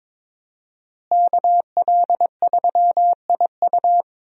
Morse code, heard as a keyed tone, about 22 wpm, 700 Hz. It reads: KL3IU
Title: USA Call Signs - 5 Characters - Repeated 3x - 22wpm